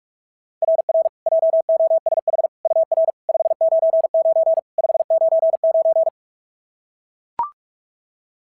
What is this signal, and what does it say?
Morse code, heard as a keyed tone, about 43 wpm, 690 Hz. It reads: RR JOSH UR 599 599
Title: SOTA QSO Practice 45wpm